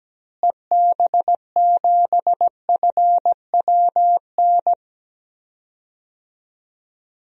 Morse code, heard as a keyed tone, about 17 wpm, 700 Hz. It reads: EB7FWN
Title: DXCC-100 Call Signs - 6 Characters - Repeated 2x - 17wpm